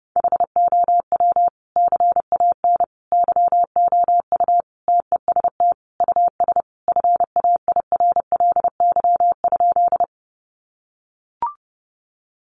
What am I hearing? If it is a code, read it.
Morse code, heard as a keyed tone, about 30 wpm, 700 Hz. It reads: HOW CAN YOU TEST US FAIRLY?
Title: Sentences from Top 1100 Words 30wpm